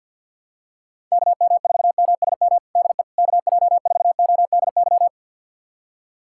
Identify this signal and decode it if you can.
Morse code, heard as a keyed tone, about 52 wpm, 700 Hz. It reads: KM4MSM DE KJ4ODY
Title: USA Callsign DE Callsign 50wpm